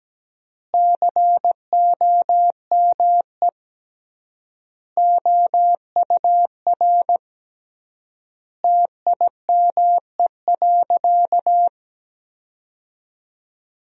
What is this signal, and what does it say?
Morse code, heard as a keyed tone, about 17 wpm, 700 Hz. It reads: COME OUR TIME.
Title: 3x Word-Spacing - Sentences from Top 200 Words - 17wpm